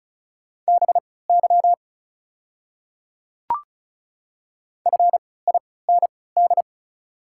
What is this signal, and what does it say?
Morse code, heard as a keyed tone, about 36 wpm, spaced as for 20 wpm, 690 Hz. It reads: BY FIND
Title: 3x Character Spacing - Top 300 Words - 35wpm